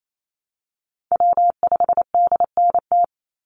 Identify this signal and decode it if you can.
Morse code, heard as a keyed tone, about 28 wpm, 700 Hz. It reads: W5DNT
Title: Call Signs Repeated 2x - 28wpm